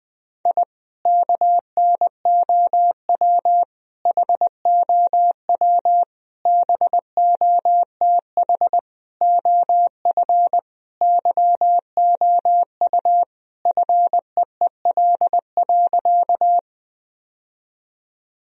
Morse code, heard as a keyed tone, about 20 wpm, 700 Hz. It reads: I KNOW HOW BOTH OF YOU FEEL.
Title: Sentences from Top 1000 Words - Review Part 1 - 20wpm